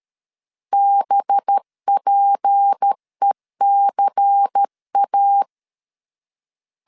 Morse code, heard as a keyed tone, about 13 wpm, 800 Hz. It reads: BPECA
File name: LEZIONI / 6 PKX0 / 6 PKX 12wpm 2024